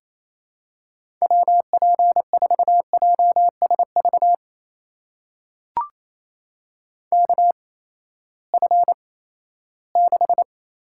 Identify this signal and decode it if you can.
Morse code, heard as a keyed tone, about 27 wpm, spaced as for 16 wpm, 700 Hz. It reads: WP4JSV KF6
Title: USA Call Signs - 6 Characters - 8x Character Spacing - 28wpm